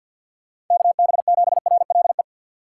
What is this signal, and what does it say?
Morse code, heard as a keyed tone, about 47 wpm, 700 Hz. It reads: KB7RLE
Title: USA Call Signs - 4x Character Spacing - 50wpm